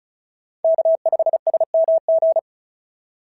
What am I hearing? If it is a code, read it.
Morse code, heard as a keyed tone, about 36 wpm, 660 Hz. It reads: K5SMG
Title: SOTA QSO Practice 35wpm